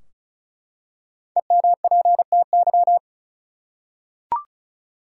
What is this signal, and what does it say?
Morse code, heard as a keyed tone, about 36 wpm, 700 Hz. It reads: EMPTY
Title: Five Letter Words - 35wpm